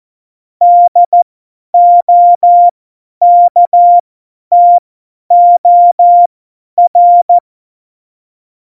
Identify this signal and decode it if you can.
Morse code, heard as a keyed tone, about 14 wpm, spaced as for 10 wpm, 700 Hz. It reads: DOKTOR
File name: Morsekurs / Witze / Witz 01. September 2020, 50 F70